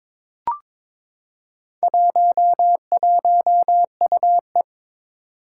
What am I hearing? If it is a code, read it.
Morse code, heard as a keyed tone, about 22 wpm, 700 Hz. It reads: 11UE
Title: Four Challenging Letter-Number Combinations 22wpm